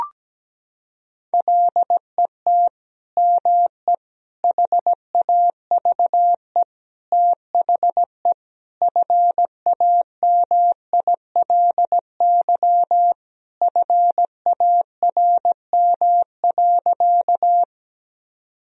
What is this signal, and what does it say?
Morse code, heard as a keyed tone, about 17 wpm, 700 Hz. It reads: LET ME HAVE THE FAMILY FARM.
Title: Sentences from Top 1000 Words - Review Part 2 - 17wpm